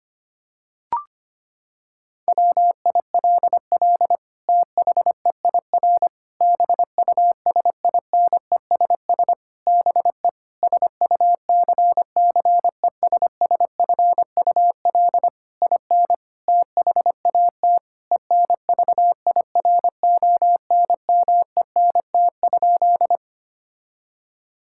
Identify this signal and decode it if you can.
Morse code, heard as a keyed tone, about 25 wpm, 700 Hz. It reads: WILL THEIR BUSINESS BE SUCCESSFUL IN THAT ENVIRONMENT?
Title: Sentences from Top 1000 Words - Review Part 5 - 25wpm